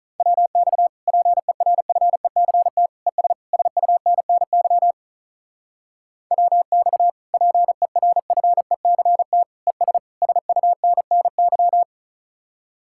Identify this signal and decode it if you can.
Morse code, heard as a keyed tone, about 46 wpm, 700 Hz. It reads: WX PERFECT ES SUNNY WX PERFECT ES SUNNY